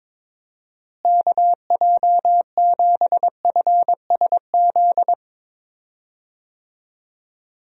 Morse code, heard as a keyed tone, about 22 wpm, 700 Hz. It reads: KJ7FSZ
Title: USA Call Signs - 6 Characters - Repeated 2x - 22wpm